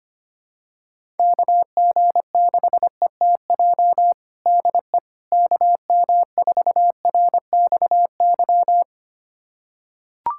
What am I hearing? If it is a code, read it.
Morse code, heard as a keyed tone, about 25 wpm, 700 Hz. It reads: KG6ETJ DE KM4RXY
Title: USA Callsign DE Callsign 25wpm